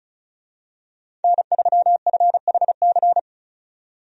Morse code, heard as a keyed tone, about 34 wpm, 700 Hz. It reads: N3FHC
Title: USA Call Signs - 5 Characters - 35wpm